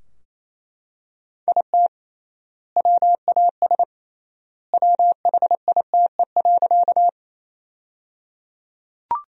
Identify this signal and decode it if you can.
Morse code, heard as a keyed tone, about 28 wpm, 700 Hz. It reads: IT WAS WHITE.
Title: 3x Word-Spacing - Sentences from Top 300 Words - 28wpm